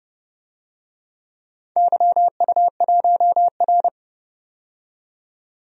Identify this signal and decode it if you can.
Morse code, heard as a keyed tone, about 31 wpm, 700 Hz. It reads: YU1R
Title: DXCC-100 Call Signs - 4 Characters - 30wpm